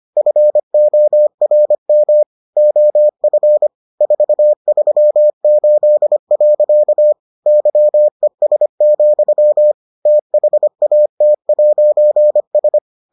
Morse code, heard as a keyed tone, about 25 wpm, 600 Hz. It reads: FORM OF 438. YES, THAT'S